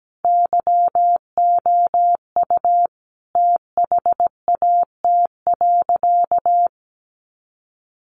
Morse code, heard as a keyed tone, about 17 wpm, 700 Hz. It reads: YOU THAT.
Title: Sentences from Top 300 Words 17wpm